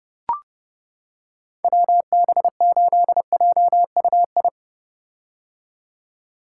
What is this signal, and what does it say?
Morse code, heard as a keyed tone, about 30 wpm, 700 Hz. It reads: WB8JUI
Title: K1USN SST QSO Practice - 30wpm